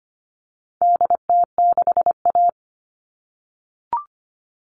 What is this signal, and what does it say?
Morse code, heard as a keyed tone, about 25 wpm, 700 Hz. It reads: DT6A
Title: All Cut Numbers - 4-Digit 25wpm